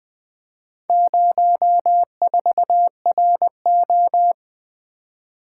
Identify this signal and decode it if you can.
Morse code, heard as a keyed tone, about 25 wpm, 700 Hz. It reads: 04RO
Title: Speed-Racing - Four Letter-Number Combinations - 20wpm